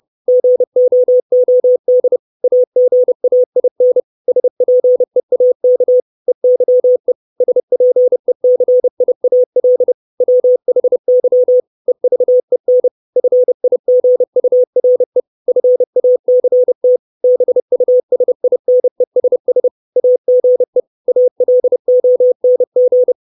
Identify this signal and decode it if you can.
Morse code, heard as a keyed tone, about 30 wpm, 500 Hz. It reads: GOOD AGAIN SPEAK EYE SPECIAL WHY EVEN FIGURE FACT BUSINESS AGE ALONG
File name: 632 Words 30WPM